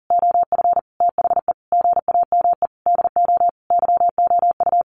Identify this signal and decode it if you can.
Morse code, heard as a keyed tone, about 40 wpm, 700 Hz. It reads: OF THE GAME DO YOU
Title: Sentences from Top 1400 Words 40wpm